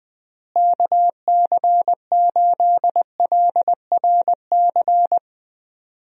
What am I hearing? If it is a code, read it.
Morse code, heard as a keyed tone, about 20 wpm, 700 Hz. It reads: KC8LRC
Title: USA Call Signs - 6 Characters - Repeated 3x - 20wpm